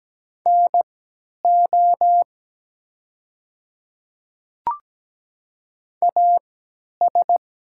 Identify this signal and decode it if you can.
Morse code, heard as a keyed tone, about 17 wpm, spaced as for 10 wpm, 700 Hz. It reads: NO AS
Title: 3x Character Spacing - Top 100 Words - 17wpm